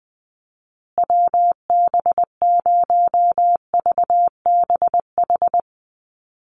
Morse code, heard as a keyed tone, about 20 wpm, 700 Hz. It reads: WB0VBH
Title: USA Call Signs Repeated 3x 20wpm